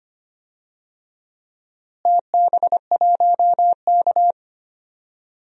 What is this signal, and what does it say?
Morse code, heard as a keyed tone, about 25 wpm, 700 Hz. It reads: TB1K
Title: DXCC-100 Call Signs - 4 Characters - Repeated 2x - 25wpm